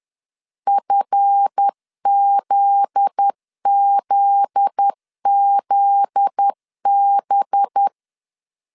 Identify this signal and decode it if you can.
Morse code, heard as a keyed tone, about 10 wpm, 800 Hz. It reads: FZZZB
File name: LEZIONI / 7 GF9Z / 7 GF9Z 10wpm 2611